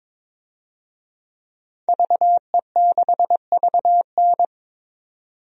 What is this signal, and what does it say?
Morse code, heard as a keyed tone, about 22 wpm, 700 Hz. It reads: VE6VN
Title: DXCC-100 Call Signs - 5 Characters - 22wpm